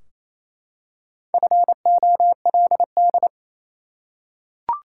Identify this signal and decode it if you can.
Morse code, heard as a keyed tone, about 28 wpm, 700 Hz. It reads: FOLD
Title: Four Letter Words - 28wpm